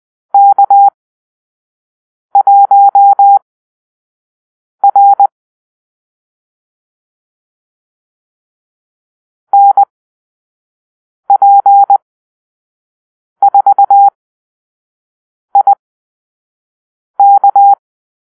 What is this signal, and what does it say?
Morse code, heard as a keyed tone, about 20 wpm, spaced as for 5 wpm, 800 Hz. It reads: K1R NP4IK
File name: arrl / Call-Sign Drill-4